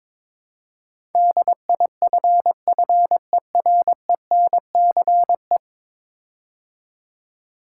Morse code, heard as a keyed tone, about 22 wpm, 700 Hz. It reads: DIFFERENCE
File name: practice / 22wpm / 10 Letter Words - First 3 Letters Given - 22wpm